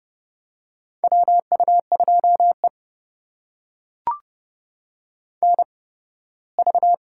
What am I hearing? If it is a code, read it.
Morse code, heard as a keyed tone, about 31 wpm, spaced as for 18 wpm, 700 Hz. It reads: WU2E NV
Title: USA Call Signs - 4 Characters - 8x Character Spacing - 30wpm